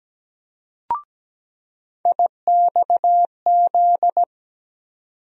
Morse code, heard as a keyed tone, about 17 wpm, 700 Hz. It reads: IXZ
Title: USA Call Sign Postfixes - 3 Letters 17wpm